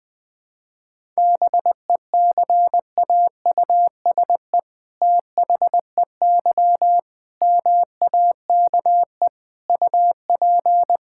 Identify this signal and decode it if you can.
Morse code, heard as a keyed tone, about 20 wpm, 700 Hz. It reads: BECAUSE THEY MAKE UP
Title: All-Age Appropriate Jokes - 20wpm